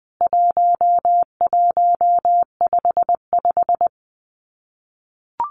Morse code, heard as a keyed tone, about 20 wpm, 700 Hz. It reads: 1155
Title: CWT Exchange 20wpm